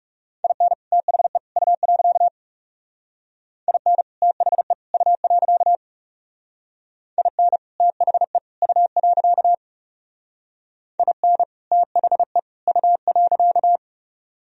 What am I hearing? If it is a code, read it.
Morse code, heard as a keyed tone, about 36 wpm, 690 Hz. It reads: IN THE U. IN THE U. IN THE U. IN THE U.
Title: Speed-Racing - Sets of 3 Words 30wpm